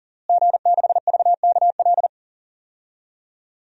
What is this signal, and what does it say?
Morse code, heard as a keyed tone, about 40 wpm, 700 Hz. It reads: G6VKL
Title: DXCC-100 Call Signs - 5 Characters - Repeated 3x - 40wpm